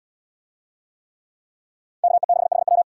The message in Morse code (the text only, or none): KE6HW